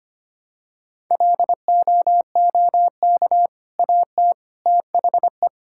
Morse code, heard as a keyed tone, about 25 wpm, 700 Hz. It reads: LOOK AT THE
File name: practice / 25wpm / Sentences with 3-Word Phrases - 25wpm